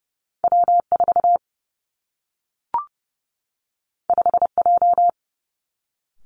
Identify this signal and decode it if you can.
Morse code, heard as a keyed tone, about 30 wpm, 700 Hz. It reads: W 4 5 J
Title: Two Challenging Letter-Number Combinations 30wpm